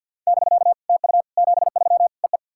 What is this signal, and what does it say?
Morse code, heard as a keyed tone, about 52 wpm, 690 Hz. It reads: <BK> TU 73 EE
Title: SOTA QSO Practice 50wpm